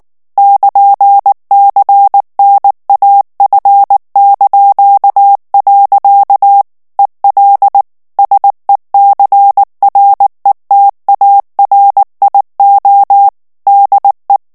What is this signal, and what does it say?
Morse code, heard as a keyed tone, about 19 wpm, 800 Hz. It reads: <KN>CNAF). EL SECRETARIO DE
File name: morse mp3 / ESPAGNOL TEXTE EN CLAIR 2 VITESSE 18 MOTS MINUTES